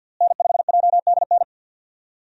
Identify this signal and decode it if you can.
Morse code, heard as a keyed tone, about 47 wpm, 700 Hz. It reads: N5JDN
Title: USA Call Signs - 5 Characters - Repeated 3x - 50wpm